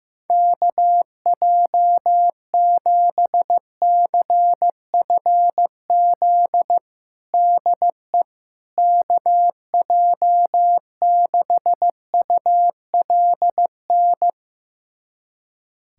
Morse code, heard as a keyed tone, about 15 wpm, 700 Hz. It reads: KJ7CFZ DE KJ6ULN
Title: USA Callsign DE Callsign - 6 Characters - 15wpm